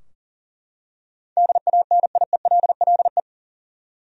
Morse code, heard as a keyed tone, about 40 wpm, 700 Hz. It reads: DANIELLE
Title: USA Common Names - 40wpm